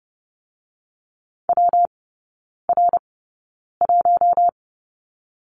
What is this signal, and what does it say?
Morse code, heard as a keyed tone, about 30 wpm, 700 Hz. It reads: W R 1
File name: Lesson 19e - Letter w - Three Characters 30wpm